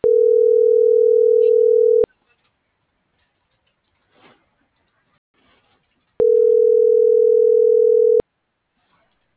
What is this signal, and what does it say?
Ringback tone (North America), 440 + 480 Hz.